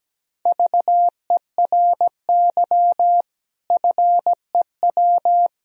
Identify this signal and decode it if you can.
Morse code, heard as a keyed tone, about 17 wpm, 700 Hz. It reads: VERY FEW
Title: Sentences from Top 1000 Words - Review Part 2 - 17wpm